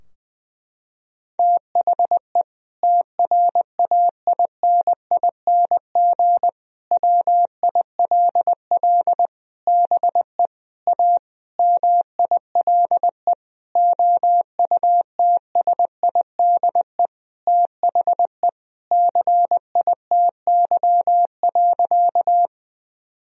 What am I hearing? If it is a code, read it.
Morse code, heard as a keyed tone, about 20 wpm, 700 Hz. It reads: THE TRAINING WILL BE A MILE OUTSIDE THE CITY.
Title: Sentences from Top 1000 Words - Review Part 4 - 20wpm